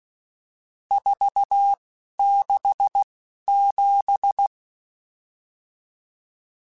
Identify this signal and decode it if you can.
Morse code, heard as a keyed tone, about 16 wpm, spaced as for 11 wpm, 800 Hz. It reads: 467
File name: RSTs / RST-mp3s / Various RST @10